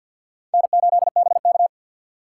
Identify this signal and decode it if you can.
Morse code, heard as a keyed tone, about 46 wpm, 700 Hz. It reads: N8BK
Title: USA Call Signs - 4 Characters - 3x Character Spacing - 50wpm